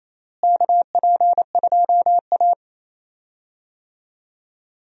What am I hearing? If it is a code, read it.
Morse code, heard as a keyed tone, about 28 wpm, 700 Hz. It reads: KP2A
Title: DXCC-100 Call Signs - 4 Characters - Repeated 3x - 28wpm